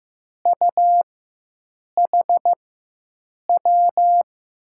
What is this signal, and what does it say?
Morse code, heard as a keyed tone, about 16 wpm, spaced as for 7 wpm, 700 Hz. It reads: UHW
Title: USA Call Signs - 4x Character Spacing - 15wpm